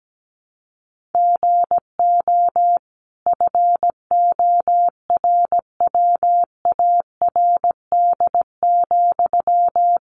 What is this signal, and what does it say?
Morse code, heard as a keyed tone, about 17 wpm, 700 Hz. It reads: GO FORWARD,
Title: Sentences from Top 1300 Words 17wpm